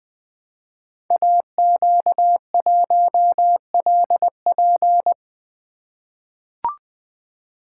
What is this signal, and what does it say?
Morse code, heard as a keyed tone, about 20 wpm, 700 Hz. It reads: AQ1LP
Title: Code Groups - 5 Letter:Numbers - 20wpm